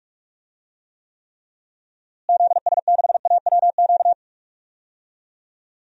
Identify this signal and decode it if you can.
Morse code, heard as a keyed tone, about 46 wpm, 700 Hz. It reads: ZS6AWQ
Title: DXCC-100 Call Signs - 6 Characters - Repeated 3x - 45wpm